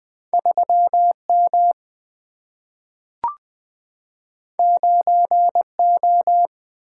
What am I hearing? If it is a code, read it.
Morse code, heard as a keyed tone, about 20 wpm, 700 Hz. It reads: 3M 9O
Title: Single-Letter Single-Number 20wpm